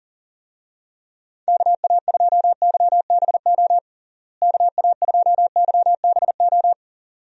Morse code, heard as a keyed tone, about 40 wpm, 700 Hz. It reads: KA2YBO KA2YBO
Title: USA Call Signs - 6 Characters - Repeated 2x - 40wpm